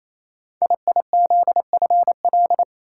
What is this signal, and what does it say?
Morse code, heard as a keyed tone, about 27 wpm, 700 Hz. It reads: IIZFL
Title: Code Groups - 5 Letters - 28wpm